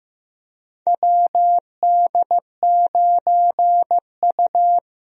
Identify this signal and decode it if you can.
Morse code, heard as a keyed tone, about 15 wpm, 700 Hz. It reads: WD9U
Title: USA Call Signs - 4 Characters - 15wpm